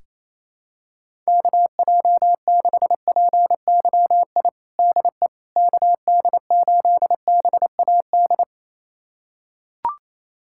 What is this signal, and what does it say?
Morse code, heard as a keyed tone, about 28 wpm, 700 Hz. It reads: KJ6PYI DE KD8BAD
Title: USA Callsign DE Callsign - 6 Characters - 28wpm